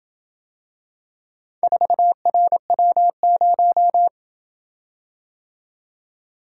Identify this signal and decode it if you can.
Morse code, heard as a keyed tone, about 26 wpm, 700 Hz. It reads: 4RW0